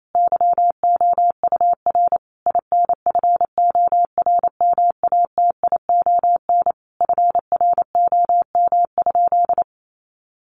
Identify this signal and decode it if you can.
Morse code, heard as a keyed tone, about 28 wpm, 700 Hz. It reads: YOUR INFORMATION FROM?
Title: Sentences from Top 600 Words 28wpm